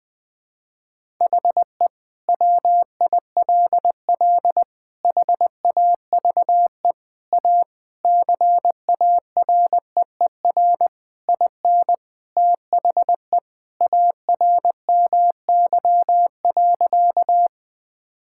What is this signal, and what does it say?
Morse code, heard as a keyed tone, about 20 wpm, 700 Hz. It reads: HE WILL HAVE A CAREER IN THE ARMY.
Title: Sentences from Top 1000 Words - Review Part 6 - 20wpm